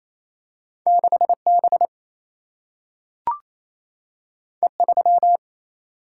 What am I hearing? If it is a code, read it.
Morse code, heard as a keyed tone, about 28 wpm, 700 Hz. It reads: 6B E3
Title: Single-Letter Single-Number 28wpm